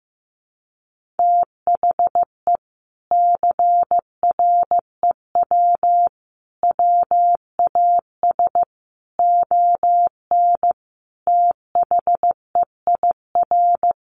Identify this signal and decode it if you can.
Morse code, heard as a keyed tone, about 15 wpm, 700 Hz. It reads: THE CREW WAS ON THEIR
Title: Sentences from Top 1500 Words 15wpm